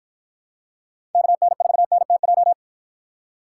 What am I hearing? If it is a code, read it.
Morse code, heard as a keyed tone, about 54 wpm, 700 Hz. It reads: KN4NTJ